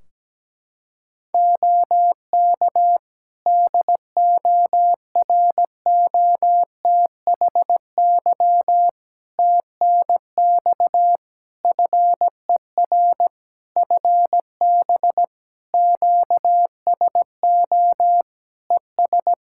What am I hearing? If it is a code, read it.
Morse code, heard as a keyed tone, about 17 wpm, 700 Hz. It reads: OK DOROTHY TNX FER FB QSO ES
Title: Practice QSO - Exchange 4 - Ending - 17wpm